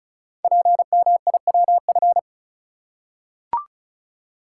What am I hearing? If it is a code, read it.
Morse code, heard as a keyed tone, about 36 wpm, 700 Hz. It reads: PMIWF
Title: Code Groups - 5 Letter:Numbers - 35wpm